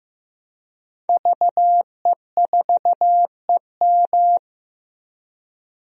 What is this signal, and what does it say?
Morse code, heard as a keyed tone, about 15 wpm, 700 Hz. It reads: VE4EM
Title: DXCC-100 Call Signs - 5 Characters - 15wpm